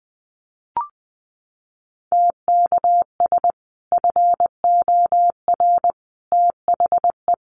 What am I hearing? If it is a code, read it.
Morse code, heard as a keyed tone, about 20 wpm, 700 Hz. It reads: TKS FOR THE
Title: Common QSO Phrases 20wpm